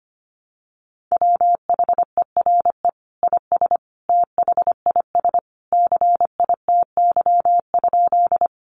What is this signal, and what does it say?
Morse code, heard as a keyed tone, about 25 wpm, 700 Hz. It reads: WHERE IS THIS CITY?
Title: Sentences from Top 300 Words 25wpm